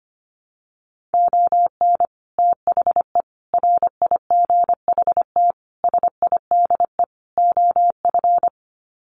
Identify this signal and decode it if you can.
Morse code, heard as a keyed tone, about 25 wpm, 700 Hz. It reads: ON THE RIGHT SIDE OF
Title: Sets of 5 Words Encore 25wpm